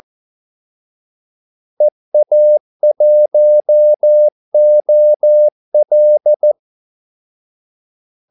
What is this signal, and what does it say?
Morse code, heard as a keyed tone, about 14 wpm, 600 Hz. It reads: EA1OL
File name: Callsigns-14wpm